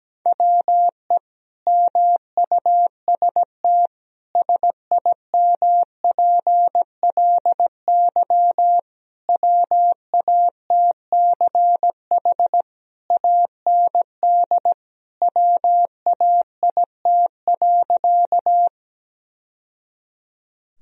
Morse code, heard as a keyed tone, about 17 wpm, 700 Hz. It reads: WE MUST SIMPLY WATCH AND WAIT.
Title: Sentences from Top 1000 Words - Review Part 4 - 17wpm